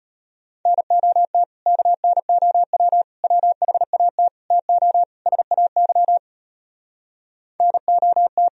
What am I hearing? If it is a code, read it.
Morse code, heard as a keyed tone, about 33 wpm, 700 Hz. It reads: NOT KNOW WHAT TO SAY NOT